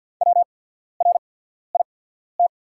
Morse code, heard as a keyed tone, about 48 wpm, spaced as for 14 wpm, 700 Hz. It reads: WRIT
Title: 8x Character Spacing - Top 300 Words - 50wpm